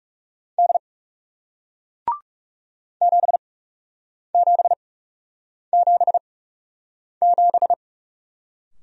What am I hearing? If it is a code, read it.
Morse code, heard as a keyed tone, about 40 wpm, spaced as for 8 wpm, 700 Hz. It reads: D 7777